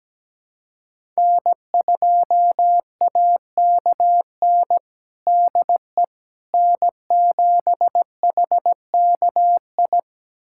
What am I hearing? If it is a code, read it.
Morse code, heard as a keyed tone, about 17 wpm, 700 Hz. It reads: N2AKN DE N7HKI
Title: USA Callsign DE Callsign - 5 Characters - 17wpm